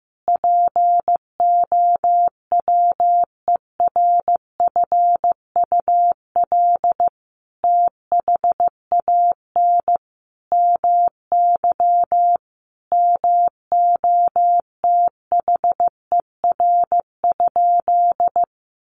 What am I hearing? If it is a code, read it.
Morse code, heard as a keyed tone, about 15 wpm, 700 Hz. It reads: POWERFUL THAN MY MOTHER?
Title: Sentences from Top 1300 Words 15wpm